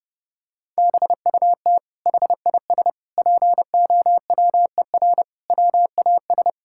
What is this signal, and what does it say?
Morse code, heard as a keyed tone, about 28 wpm, 700 Hz. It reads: BUT HIS POWER WAS
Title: Sentences from Top 1000 Words - Review Part 6 - 30wpm